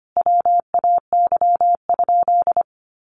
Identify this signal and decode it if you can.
Morse code, heard as a keyed tone, about 25 wpm, 700 Hz. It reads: WAY?
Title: Sentences from Top 1100 Words 25wpm